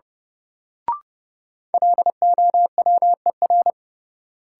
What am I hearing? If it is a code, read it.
Morse code, heard as a keyed tone, about 30 wpm, 700 Hz. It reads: LOWER
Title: Sentences from Top 700 Words - New Word Drill - 30wpm